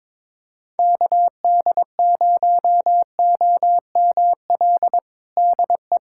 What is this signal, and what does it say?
Morse code, heard as a keyed tone, about 22 wpm, 700 Hz. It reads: KD0OML DE
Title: USA Callsign DE Callsign 22wpm